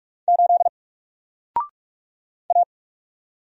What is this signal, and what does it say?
Morse code, heard as a keyed tone, about 46 wpm, 700 Hz. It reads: 8 A
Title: Single Letter-Number 45wpm